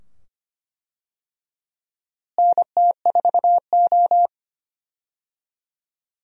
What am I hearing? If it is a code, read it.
Morse code, heard as a keyed tone, about 25 wpm, 700 Hz. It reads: NT4O
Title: USA Call Signs - 4 Characters - Repeated 2x - 25wpm